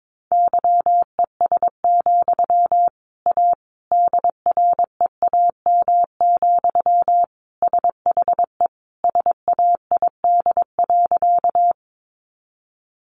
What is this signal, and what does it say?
Morse code, heard as a keyed tone, about 22 wpm, 700 Hz. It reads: YES, A DREAM, SHE SAID.
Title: Sentences from Top 1500 Words 22wpm